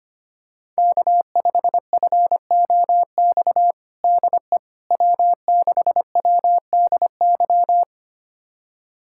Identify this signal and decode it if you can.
Morse code, heard as a keyed tone, about 25 wpm, 700 Hz. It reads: K5FOX DE W6WDY
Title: USA Callsign DE Callsign - 5 Characters - 25wpm